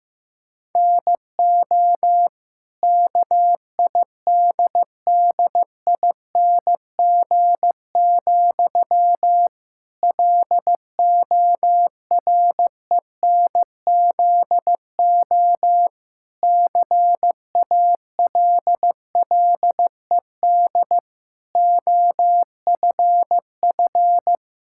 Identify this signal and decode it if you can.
Morse code, heard as a keyed tone, about 15 wpm, 700 Hz. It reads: NO KIDDING, LORENZO CALLED OFF
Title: Pangram Challenge v2 - 15wpm